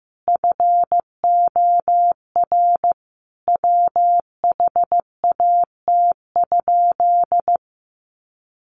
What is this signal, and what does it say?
Morse code, heard as a keyed tone, about 15 wpm, 700 Hz. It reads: FOR WHAT?
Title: Sentences from Top 700 Words 15wpm